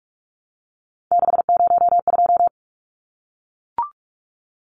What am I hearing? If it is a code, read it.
Morse code, heard as a keyed tone, about 44 wpm, 700 Hz. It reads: 602
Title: Numbers - 3-Digit 45wpm